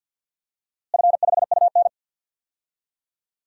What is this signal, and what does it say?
Morse code, heard as a keyed tone, about 51 wpm, 700 Hz. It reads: F5UN
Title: DXCC-100 Call Signs - 4 Characters - Repeated 2x - 50wpm